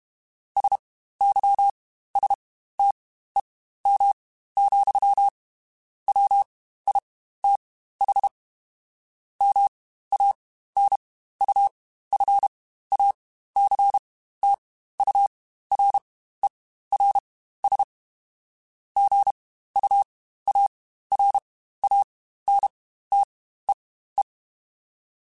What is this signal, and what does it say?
Morse code, heard as a keyed tone, about 32 wpm, spaced as for 15 wpm, 800 Hz. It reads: SYSTEM, WITH MANUFACTURERS GUARANTEE
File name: Quotes Spaced / Quotes Spaced-mp3s / Airtight sp @30